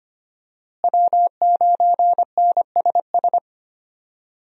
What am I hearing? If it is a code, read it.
Morse code, heard as a keyed tone, about 25 wpm, 700 Hz. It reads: W9NSS
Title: USA Call Signs - 5 Characters - Repeated 3x - 25wpm